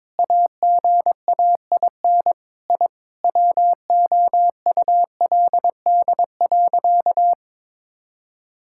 Morse code, heard as a keyed tone, about 22 wpm, 700 Hz. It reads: AGAIN I WOULD.
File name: Sentences with 9-Word Phrases - 22wpm